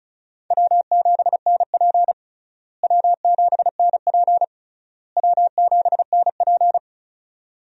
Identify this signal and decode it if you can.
Morse code, heard as a keyed tone, about 36 wpm, 700 Hz. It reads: W7NP W7NP W7NP
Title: USA Call Signs - 4 Characters - Repeated 3x - 35wpm